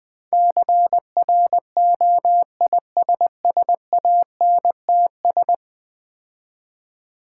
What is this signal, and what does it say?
Morse code, heard as a keyed tone, about 20 wpm, 700 Hz. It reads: CROISSANTS
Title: Difficult Words with Hints - 20wpm